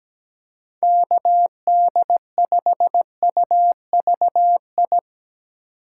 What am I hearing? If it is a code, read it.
Morse code, heard as a keyed tone, about 17 wpm, 700 Hz. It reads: KD5UVI
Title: USA Call Signs - 6 Characters - 3x Character Spacing - 17wpm